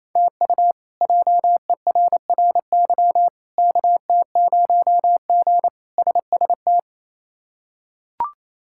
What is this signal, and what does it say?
Morse code, heard as a keyed tone, about 28 wpm, 700 Hz. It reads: TU JERRY KT0G SST
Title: K1USN SST QSO Practice - 28wpm